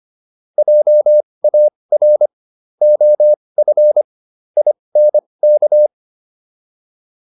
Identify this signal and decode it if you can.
Morse code, heard as a keyed tone, about 25 wpm, spaced as for 20 wpm, 600 Hz. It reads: JAR OF INK